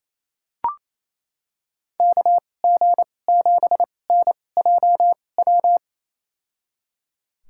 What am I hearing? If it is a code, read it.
Morse code, heard as a keyed tone, about 28 wpm, spaced as for 20 wpm, 700 Hz. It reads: KG7NJW
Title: USA Call Signs - 6 Characters - 2x Character Spacing - 28wpm